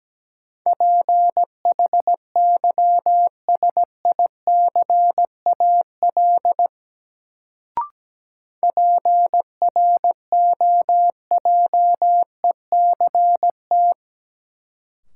Morse code, heard as a keyed tone, about 17 wpm, 700 Hz. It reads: PHYSICAL PROJECT
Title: Sentences from Top 1400 Words - New Word Drill Alt - 17wpm